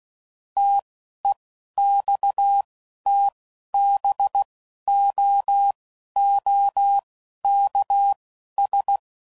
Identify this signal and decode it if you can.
Morse code, heard as a keyed tone, about 16 wpm, spaced as for 11 wpm, 800 Hz. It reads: TEXTBOOKS
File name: CW Code Groups @10WPM / Technical Subjects / Gates Intro